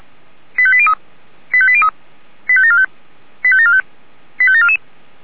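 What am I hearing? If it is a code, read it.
Five-tone selective call (ZVEI), heard as quick runs of five tones, tones of about 70 ms. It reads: 85803 85803 85745 85745 85744
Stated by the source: Five-tone ZVEI selective call 85726 received by SDR.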